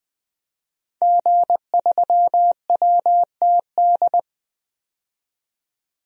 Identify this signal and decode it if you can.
Morse code, heard as a keyed tone, about 21 wpm, 700 Hz. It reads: G3WTD
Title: DXCC-100 Call Signs - 5 Characters - 20wpm